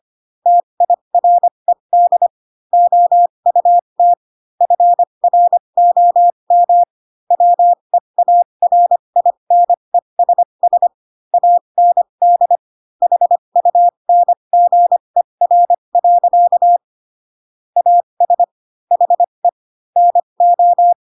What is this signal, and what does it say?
Morse code, heard as a keyed tone, about 25 wpm, spaced as for 22 wpm, 700 Hz. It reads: TIRED OUT FROM WEARINESS AND HUNGER. AS HE NO
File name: Pinocchio / Pinocchio ch6 18 25 ik1hge